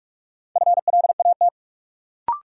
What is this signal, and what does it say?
Morse code, heard as a keyed tone, about 46 wpm, 700 Hz. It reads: FLAT
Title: Sentences from Top 1000 Words - New Word Drill - 45wpm